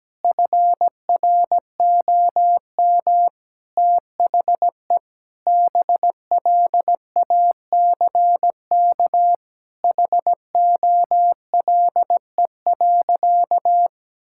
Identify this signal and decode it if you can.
Morse code, heard as a keyed tone, about 17 wpm, 700 Hz. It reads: FROM THE BLACK HOLE.
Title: Sentences from Top 1600 Words 17wpm